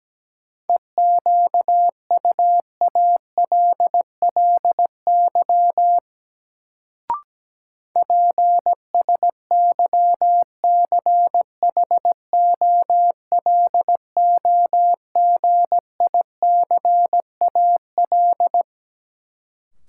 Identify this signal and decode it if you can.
Morse code, heard as a keyed tone, about 17 wpm, 700 Hz. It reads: EQUALLY PSYCHOLOGICAL
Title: Sentences from Top 1400 Words - New Word Drill - 17wpm